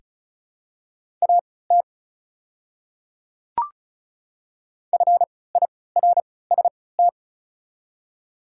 Morse code, heard as a keyed tone, about 36 wpm, spaced as for 20 wpm, 690 Hz. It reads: AT FIRST
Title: 3x Character Spacing - Top 200 Words - 35wpm